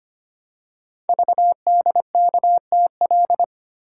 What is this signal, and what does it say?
Morse code, heard as a keyed tone, about 25 wpm, 700 Hz. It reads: VDKTL
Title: Code Groups - 5 Letters - 25wpm